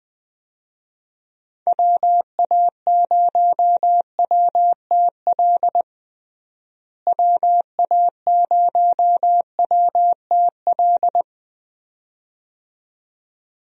Morse code, heard as a keyed tone, about 21 wpm, 700 Hz. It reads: WA0WTL WA0WTL
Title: USA Call Signs - 6 Characters - Repeated 2x - 20wpm